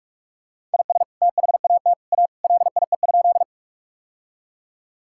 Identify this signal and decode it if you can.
Morse code, heard as a keyed tone, about 46 wpm, 700 Hz. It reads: IS THAT A LIE?
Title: Sentences from Top 1600 Words 45wpm